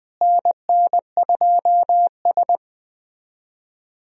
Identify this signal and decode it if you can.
Morse code, heard as a keyed tone, about 21 wpm, 700 Hz. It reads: NN2S
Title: USA Call Signs - 4 Characters - Repeated 3x - 20wpm